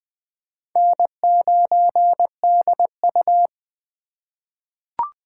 Morse code, heard as a keyed tone, about 20 wpm, 700 Hz. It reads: N9DU
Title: USA Call Signs - 4 Characters - 3x Character Spacing - 20wpm